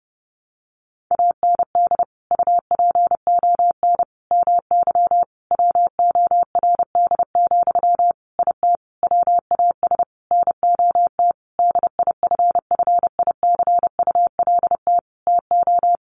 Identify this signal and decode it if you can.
Morse code, heard as a keyed tone, about 30 wpm, 700 Hz. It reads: AND UPON MY WORD, IT WAS NOT DIFFICULT TO
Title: Sentences from Top 1100 Words 30wpm